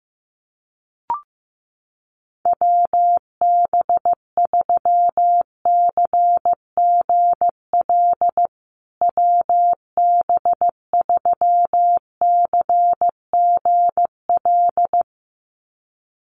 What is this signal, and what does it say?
Morse code, heard as a keyed tone, about 15 wpm, 700 Hz. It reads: WB3CGL WB3CGL
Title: USA Call Signs Repeated 2x 15wpm